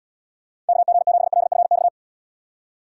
Morse code, heard as a keyed tone, about 76 wpm, 700 Hz. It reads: KG7LFX